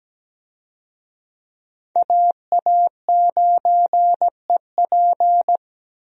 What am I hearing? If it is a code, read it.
Morse code, heard as a keyed tone, about 17 wpm, 700 Hz. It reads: AA9EP
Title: USA Call Signs - 5 Characters - 17wpm